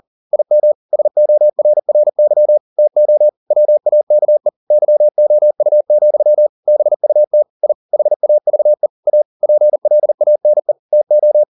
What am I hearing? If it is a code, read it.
Morse code, heard as a keyed tone, about 40 wpm, 600 Hz. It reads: IM SORRY TO WAKE YOU, BUT I HAVE A PLANE TO